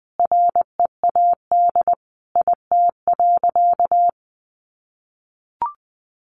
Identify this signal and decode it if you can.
Morse code, heard as a keyed tone, about 20 wpm, 700 Hz. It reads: READ IT.
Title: Sentences from Top 200 Words 20wpm